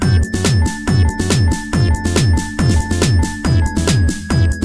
There is no code.